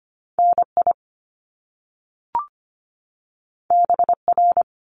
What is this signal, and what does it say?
Morse code, heard as a keyed tone, about 25 wpm, 700 Hz. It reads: NI BR
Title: Top Two Letter Combinations in Words 25wpm